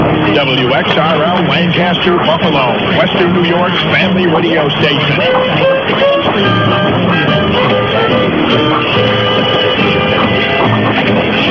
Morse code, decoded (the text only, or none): none